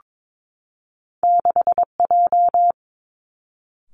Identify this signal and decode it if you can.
Morse code, heard as a keyed tone, about 22 wpm, 700 Hz. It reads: 6J
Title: Two Challenging Letter-Number Combinations 22wpm